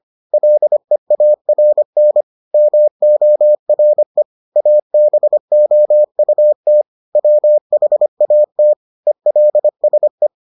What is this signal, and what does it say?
Morse code, heard as a keyed tone, about 25 wpm, 600 Hz. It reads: LEARN MORE ABOUT WHAT ELSE